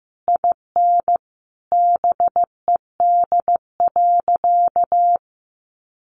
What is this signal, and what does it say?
Morse code, heard as a keyed tone, about 15 wpm, 700 Hz. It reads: IN BED.
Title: Sentences from Top 1200 Words 15wpm